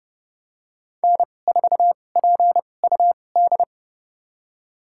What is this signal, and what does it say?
Morse code, heard as a keyed tone, about 31 wpm, spaced as for 22 wpm, 700 Hz. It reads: N4PUD
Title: USA Call Signs - 2x Character Spacing - 30wpm